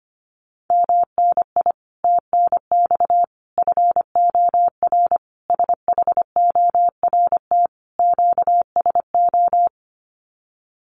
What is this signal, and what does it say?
Morse code, heard as a keyed tone, about 25 wpm, 700 Hz. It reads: MNI TNX FOR SHORT QSO
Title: Common QSO Phrases 25wpm